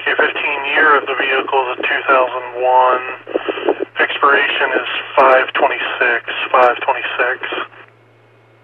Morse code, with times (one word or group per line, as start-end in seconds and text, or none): none